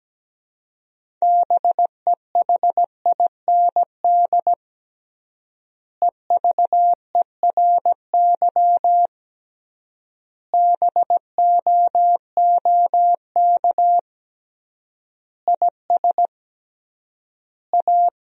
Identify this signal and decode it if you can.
Morse code, heard as a keyed tone, about 17 wpm, 700 Hz. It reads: BEHIND EVERY BOOK IS A
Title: 3x Word-Spacing - Sentences from Top 500 Words - 17wpm